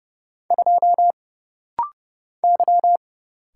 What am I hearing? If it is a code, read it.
Morse code, heard as a keyed tone, about 30 wpm, 700 Hz. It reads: 2 Y
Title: Single Letter-Number - Rapid-Fire 30wpm